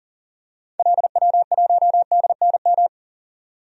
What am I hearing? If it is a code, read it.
Morse code, heard as a keyed tone, about 40 wpm, 700 Hz. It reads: LW1DNM
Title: DXCC-100 Call Signs - 6 Characters - 40wpm